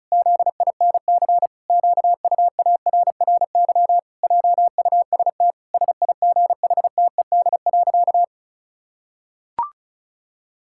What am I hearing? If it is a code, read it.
Morse code, heard as a keyed tone, about 36 wpm, 700 Hz. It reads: ZINC QUARRY JUST SIGHTED.
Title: Pangram Challenge v2 - 35wpm